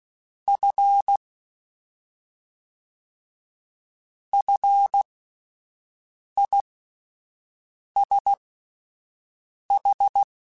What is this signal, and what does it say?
Morse code, heard as a keyed tone, about 16 wpm, spaced as for 5 wpm, 800 Hz. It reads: F FISH
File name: Fs / Single F mp3s / Single F Prefixes 3x2 Twice @5